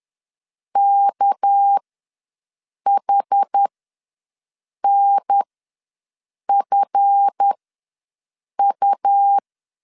Morse code, heard as a keyed tone, about 10 wpm, spaced as for 6 wpm, 800 Hz. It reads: KHNFU
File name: LEZIONI / 7 GF9Z / 7 GF ws 10wpm 3918